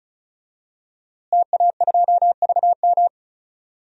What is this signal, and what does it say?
Morse code, heard as a keyed tone, about 36 wpm, 700 Hz. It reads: TA2VM
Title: DXCC-100 Call Signs - 5 Characters - Repeated 3x - 35wpm